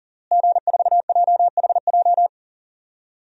Morse code, heard as a keyed tone, about 40 wpm, 700 Hz. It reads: G4JHJ
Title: DXCC-100 Call Signs - 5 Characters - Repeated 3x - 40wpm